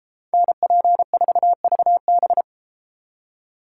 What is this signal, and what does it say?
Morse code, heard as a keyed tone, about 33 wpm, 700 Hz. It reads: NP4VB